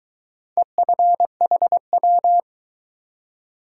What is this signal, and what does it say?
Morse code, heard as a keyed tone, about 22 wpm, 700 Hz. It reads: EFHW